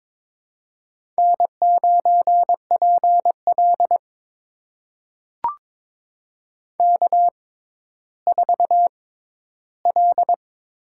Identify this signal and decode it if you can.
Morse code, heard as a keyed tone, about 22 wpm, spaced as for 13 wpm, 700 Hz. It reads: N9PL K4L
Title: USA Call Signs - 6x Character Spacing - 22wpm